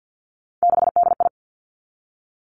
Morse code, heard as a keyed tone, about 50 wpm, 700 Hz. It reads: 6DI
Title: Three Challenging Letter-Number Combinations 50wpm